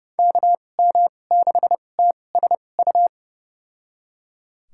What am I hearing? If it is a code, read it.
Morse code, heard as a keyed tone, about 32 wpm, spaced as for 22 wpm, 700 Hz. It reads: KM6TSU
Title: USA Call Signs - 6 Characters - 2x Character Spacing - 30wpm